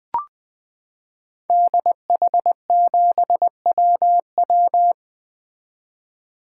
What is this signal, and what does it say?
Morse code, heard as a keyed tone, about 20 wpm, 700 Hz. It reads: DH7WW
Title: 2023 SOTA Activator Call Signs - 20wpm